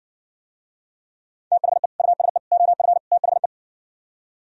Morse code, heard as a keyed tone, about 59 wpm, 700 Hz. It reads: THE USE OF THE